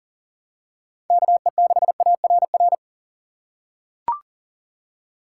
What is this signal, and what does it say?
Morse code, heard as a keyed tone, about 40 wpm, 700 Hz. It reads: KE6ARR
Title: USA Call Signs - 6 Characters - 6x Character Spacing - 40wpm